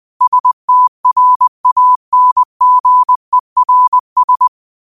Morse code, heard as a keyed tone, about 20 wpm, 1000 Hz. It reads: STRANGERS
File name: qotdfiles / 20wpm